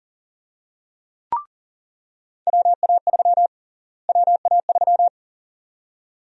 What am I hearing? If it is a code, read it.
Morse code, heard as a keyed tone, about 40 wpm, 700 Hz. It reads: WA3 WA3
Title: USA Call Sign Prefixes - Repeated 2x - 40wpm